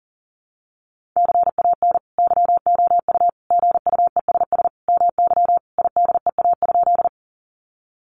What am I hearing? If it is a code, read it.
Morse code, heard as a keyed tone, about 38 wpm, 700 Hz. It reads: CAN YOU GUESS MY IDEA?
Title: Sentences from Top 1500 Words 40wpm